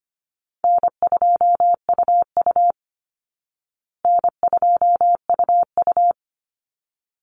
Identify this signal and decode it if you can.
Morse code, heard as a keyed tone, about 25 wpm, 700 Hz. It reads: N2UU N2UU
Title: Easy Call Signs Repeated 2x - 25wpm